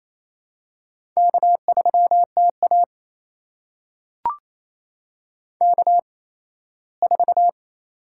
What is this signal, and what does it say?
Morse code, heard as a keyed tone, about 28 wpm, spaced as for 17 wpm, 700 Hz. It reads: K3TA K4
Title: USA Call Signs - 4 Characters - 8x Character Spacing - 28wpm